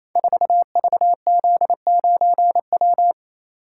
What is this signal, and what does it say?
Morse code, heard as a keyed tone, about 27 wpm, 700 Hz. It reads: 4VZ9W
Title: Code Groups - 5 Letter:Numbers - 28wpm